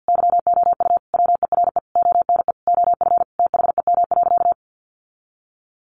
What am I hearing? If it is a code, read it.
Morse code, heard as a keyed tone, about 48 wpm, 700 Hz. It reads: YOU WERE ONE OF THEM.
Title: Sentences from Top 1500 Words 50wpm